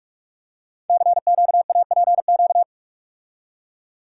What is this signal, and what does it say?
Morse code, heard as a keyed tone, about 44 wpm, 700 Hz. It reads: CQAPQ
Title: Code Groups - 5 Letters - 45wpm